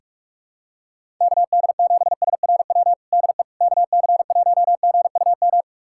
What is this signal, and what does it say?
Morse code, heard as a keyed tone, about 47 wpm, 700 Hz. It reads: KD7SRW DE KC1GUM
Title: USA Callsign DE Callsign - 6 Characters - 45wpm